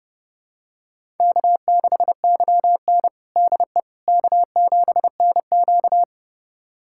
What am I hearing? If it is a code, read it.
Morse code, heard as a keyed tone, about 30 wpm, 700 Hz. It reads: K6YN DE K7NQ
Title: USA Callsign DE Callsign - 4 Characters - 30wpm